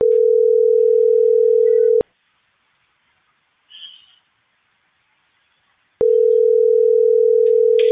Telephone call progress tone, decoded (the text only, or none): ringback tone